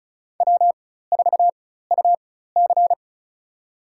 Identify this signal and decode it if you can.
Morse code, heard as a keyed tone, about 34 wpm, spaced as for 16 wpm, 700 Hz. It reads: W4UC
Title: USA Call Signs - 4x Character Spacing - 35wpm